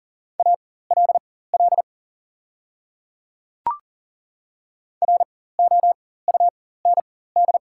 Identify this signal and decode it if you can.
Morse code, heard as a keyed tone, about 40 wpm, spaced as for 19 wpm, 700 Hz. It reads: ALL ROUND
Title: 4x Character Spacing - Top 200 Words - 40wpm